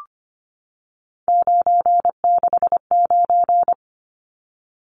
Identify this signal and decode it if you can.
Morse code, heard as a keyed tone, about 25 wpm, 700 Hz. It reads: 969
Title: Numbers - 3-Digit 25wpm